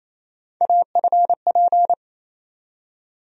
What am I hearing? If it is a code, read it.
Morse code, heard as a keyed tone, about 28 wpm, 700 Hz. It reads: AFP
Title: USA Call Sign Postfixes - 3 Letters 28wpm